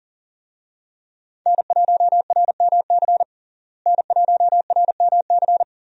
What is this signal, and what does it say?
Morse code, heard as a keyed tone, about 40 wpm, 700 Hz. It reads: N1RMC N1RMC
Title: USA Call Signs - 5 Characters - Repeated 3x - 40wpm